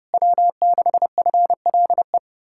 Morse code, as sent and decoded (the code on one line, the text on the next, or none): .-- -.... ..-. .-.. .
W6FLE